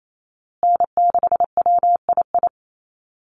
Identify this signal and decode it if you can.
Morse code, heard as a keyed tone, about 28 wpm, 700 Hz. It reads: N6WII
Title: USA Call Signs Repeated 3x 28wpm